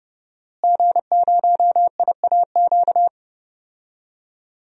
Morse code, heard as a keyed tone, about 30 wpm, 700 Hz. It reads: G0IAQ
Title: DXCC-100 Call Signs - 5 Characters - Repeated 3x - 30wpm